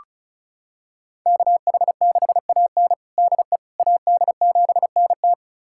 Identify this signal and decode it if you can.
Morse code, heard as a keyed tone, about 36 wpm, 700 Hz. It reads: KH6AN DE AD7NT
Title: USA Callsign DE Callsign 35wpm